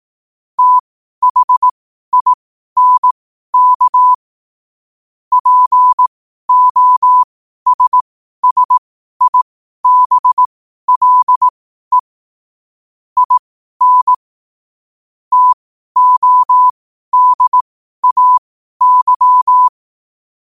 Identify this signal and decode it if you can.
Morse code, heard as a keyed tone, about 18 wpm, spaced as for 12 wpm, 1000 Hz. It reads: THINK POSSIBLE IN TODAY
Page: Quotes for Fri, 15 Aug 2025 in Morse Code at 12 words per minute.